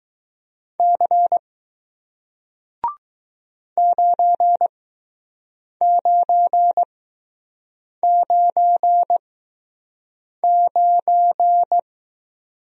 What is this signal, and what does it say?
Morse code, heard as a keyed tone, about 20 wpm, spaced as for 6 wpm, 700 Hz. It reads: C 9999